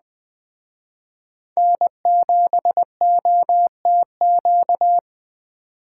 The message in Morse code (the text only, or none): N7OTQ